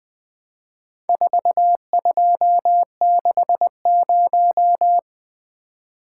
Morse code, heard as a keyed tone, about 20 wpm, 700 Hz. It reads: 4260
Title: Four Letter-Number Combinations - 20wpm